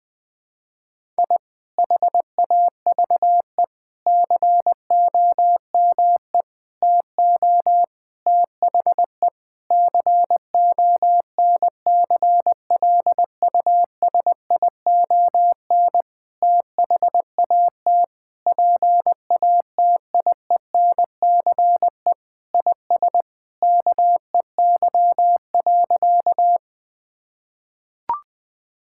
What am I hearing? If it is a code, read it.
Morse code, heard as a keyed tone, about 20 wpm, 700 Hz. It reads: I HAVE COME TO THE CONCLUSION THAT PATIENCE IS KEY.
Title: Sentences with 7-Word Phrases - 20wpm